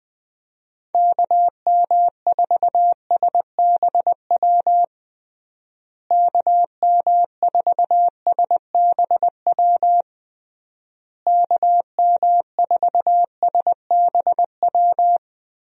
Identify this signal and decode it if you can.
Morse code, heard as a keyed tone, about 20 wpm, 700 Hz. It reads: KM4SBW KM4SBW KM4SBW
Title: USA Call Signs - 6 Characters - Repeated 3x - 20wpm